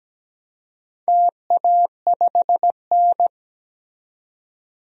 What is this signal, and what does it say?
Morse code, heard as a keyed tone, about 17 wpm, 700 Hz. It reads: TA5N
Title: DXCC-100 Call Signs - 4 Characters - 17wpm